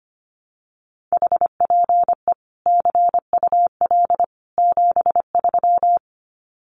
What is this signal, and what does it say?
Morse code, heard as a keyed tone, about 25 wpm, 700 Hz. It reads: HPE CUL 73
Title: Common QSO Phrases 25wpm